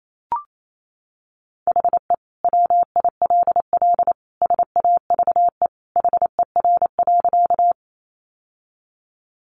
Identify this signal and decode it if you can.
Morse code, heard as a keyed tone, about 28 wpm, 700 Hz. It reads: HE WILL SAVE HER.
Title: Sentences from Top 1100 Words 28wpm